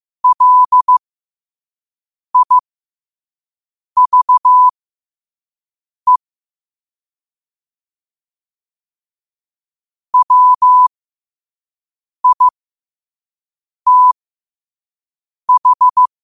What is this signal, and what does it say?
Morse code, heard as a keyed tone, about 15 wpm, spaced as for 5 wpm, 1000 Hz. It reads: LIVE WITH
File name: qotdfiles / 5wpm